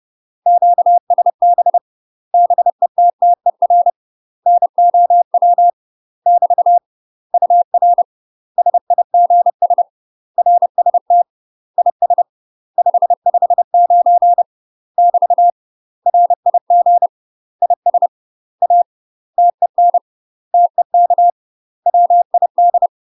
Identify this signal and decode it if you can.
Morse code, heard as a keyed tone, about 30 wpm, 700 Hz. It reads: QSB BETTER NOW <BT> UR SIGS RST IS 559 <BT> RIG IS A TEN TEK WID
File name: UK MASTER QSOs TRAINING / 30WPM / UK MASTER QSOs 0040